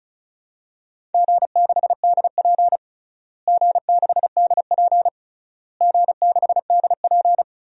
Morse code, heard as a keyed tone, about 36 wpm, 700 Hz. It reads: G6DP G6DP G6DP
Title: DXCC-100 Call Signs - 4 Characters - Repeated 3x - 35wpm